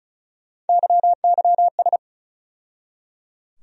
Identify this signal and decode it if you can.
Morse code, heard as a keyed tone, about 36 wpm, 700 Hz. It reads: YYS
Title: USA Call Sign Postfixes - 3 Letters 35wpm